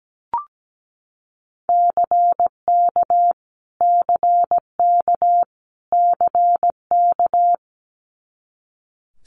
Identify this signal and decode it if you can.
Morse code, heard as a keyed tone, about 17 wpm, 700 Hz. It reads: CK CK CK
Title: Multi-letter Phonemes Repeated 3x 17wpm